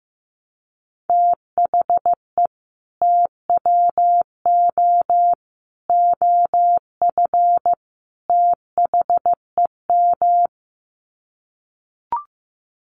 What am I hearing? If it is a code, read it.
Morse code, heard as a keyed tone, about 15 wpm, 700 Hz. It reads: THE TWO OF THEM
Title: Sets of 4 Words 15wpm